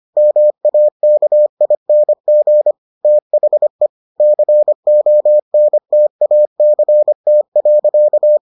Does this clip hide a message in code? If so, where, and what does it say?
Yes: Morse code from 0.2 to 8.4 s, heard as a keyed tone, about 25 wpm, 600 Hz. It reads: MAKING THE CONTACT.